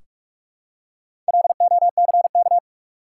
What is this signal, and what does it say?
Morse code, heard as a keyed tone, about 47 wpm, 700 Hz. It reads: LOCK
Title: Four Letter Words - 45wpm